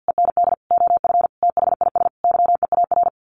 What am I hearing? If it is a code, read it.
Morse code, heard as a keyed tone, about 50 wpm, 700 Hz. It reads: END OF THIS YEAR
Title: Sets of 5 Words Encore 50wpm